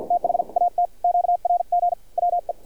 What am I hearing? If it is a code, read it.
Morse code, heard as a keyed tone, about 51 wpm, spaced as for 43 wpm, 700 Hz, with radio noise. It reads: THAT QRG WE
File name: archived / CRSnet